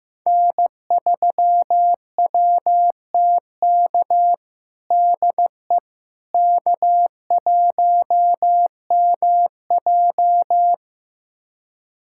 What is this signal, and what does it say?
Morse code, heard as a keyed tone, about 16 wpm, 700 Hz. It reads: N3WTK DE K1MJ
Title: USA Callsign DE Callsign 15wpm